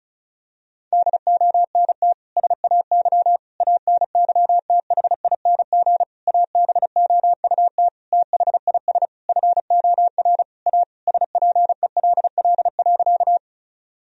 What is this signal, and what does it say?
Morse code, heard as a keyed tone, about 34 wpm, 700 Hz. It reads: DONT SAY ANYTHING ABOUT THIS FOR A SPELL.
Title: Sentences from Top 1000 Words - Review Part 1 - 35wpm